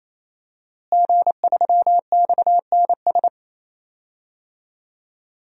Morse code, heard as a keyed tone, about 28 wpm, 700 Hz. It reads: G3XNS
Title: DXCC-100 Call Signs - 5 Characters - Repeated 3x - 28wpm